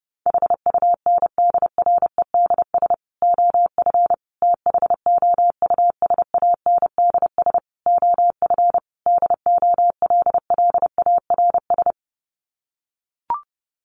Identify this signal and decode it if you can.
Morse code, heard as a keyed tone, about 30 wpm, 700 Hz. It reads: HUNDREDS OF THOUSANDS OF DOLLARS
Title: Sets of 5 Words Encore 30wpm